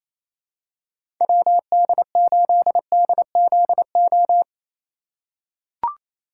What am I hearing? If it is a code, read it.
Morse code, heard as a keyed tone, about 28 wpm, 700 Hz. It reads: WD8DZO
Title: USA Call Signs - 4x Character Spacing - 28wpm